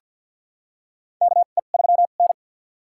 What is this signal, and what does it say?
Morse code, heard as a keyed tone, about 48 wpm, spaced as for 35 wpm, 700 Hz. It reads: KE3N
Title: USA Call Signs - 4 Characters - 2x Character Spacing - 50wpm